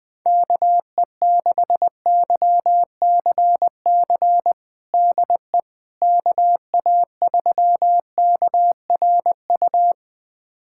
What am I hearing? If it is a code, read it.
Morse code, heard as a keyed tone, about 21 wpm, 700 Hz. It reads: KE6YCC DE KA3KRU
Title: USA Callsign DE Callsign 20wpm